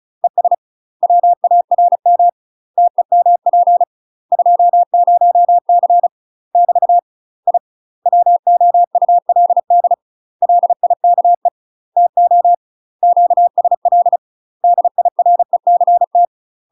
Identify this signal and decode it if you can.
Morse code, heard as a keyed tone, about 35 wpm, 700 Hz. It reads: ES WARM TEMP 20C <BT> I WOULD LIKE TO QSL DIRECT
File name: UK MASTER QSOs TRAINING / 35WPM / UK MASTER QSOs 0035